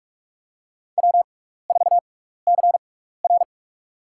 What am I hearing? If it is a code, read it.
Morse code, heard as a keyed tone, about 46 wpm, spaced as for 15 wpm, 690 Hz. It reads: W4CR
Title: USA Call Signs - 6x Character Spacing - 45wpm